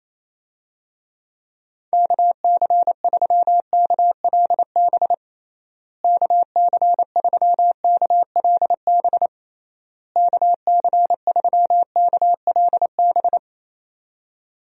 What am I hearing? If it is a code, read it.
Morse code, heard as a keyed tone, about 28 wpm, 700 Hz. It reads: KC3KLB KC3KLB KC3KLB
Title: USA Call Signs - 6 Characters - Repeated 3x - 28wpm